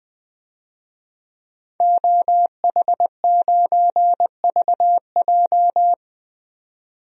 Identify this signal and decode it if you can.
Morse code, heard as a keyed tone, about 20 wpm, 700 Hz. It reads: OH9VJ
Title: DXCC-100 Call Signs - 5 Characters - 20wpm